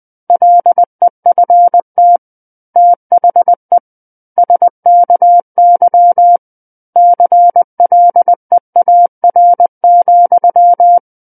Morse code, heard as a keyed tone, about 20 wpm, 700 Hz. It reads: LEFT THE SKY CLEAR,
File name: 20 WPM